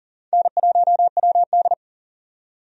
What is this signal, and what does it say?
Morse code, heard as a keyed tone, about 40 wpm, 700 Hz. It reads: N1WD
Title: USA Call Signs - 4 Characters - 40wpm